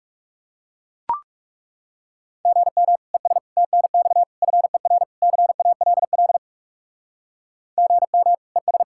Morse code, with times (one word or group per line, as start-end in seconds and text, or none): 2.5-3.0 GM
3.1-3.4 ES
3.6-4.2 TNX
4.4-5.0 FER
5.2-6.4 CALL
7.8-8.3 GM
8.6-8.8 ES